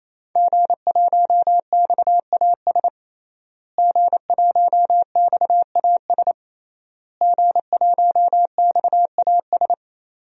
Morse code, heard as a keyed tone, about 28 wpm, 700 Hz. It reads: G1XAS G1XAS G1XAS
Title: DXCC-100 Call Signs - 5 Characters - Repeated 3x - 28wpm